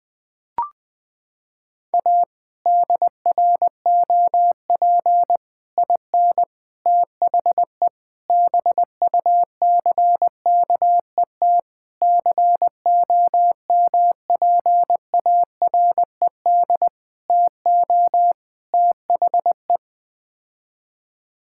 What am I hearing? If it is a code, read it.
Morse code, heard as a keyed tone, about 20 wpm, 700 Hz. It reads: A DROP IN THE BUCKET COMPARED TO THE
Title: Sets of 8 Words - 20wpm